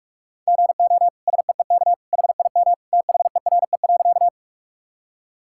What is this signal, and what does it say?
Morse code, heard as a keyed tone, about 48 wpm, 700 Hz. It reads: GO SEEK HIM THERE.
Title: Sentences from Top 1600 Words 45wpm